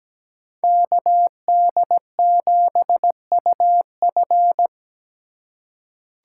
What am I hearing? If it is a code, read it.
Morse code, heard as a keyed tone, about 17 wpm, 700 Hz. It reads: KD7UF
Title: USA Call Signs - 5 Characters - 3x Character Spacing - 17wpm